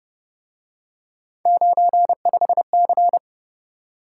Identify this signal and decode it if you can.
Morse code, heard as a keyed tone, about 29 wpm, 700 Hz. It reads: 95C